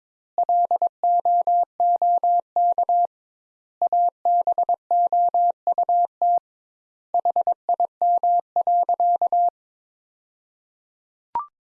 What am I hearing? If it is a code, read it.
Morse code, heard as a keyed tone, about 22 wpm, 700 Hz. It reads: LOOK ABOUT HIM.
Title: Sentences Top 200 Words - 22wpm